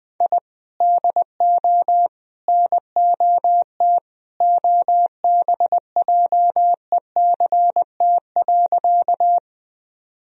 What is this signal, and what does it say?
Morse code, heard as a keyed tone, about 20 wpm, 700 Hz. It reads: I DO NOT OBJECT.
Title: Sentences from Top 1000 Words - Review Part 3 - 20wpm